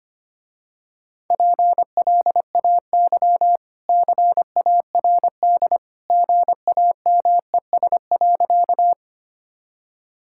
Morse code, heard as a keyed tone, about 25 wpm, 700 Hz. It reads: PLAY CARD GAMES.
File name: Easy Sentences with Hints - 25wpm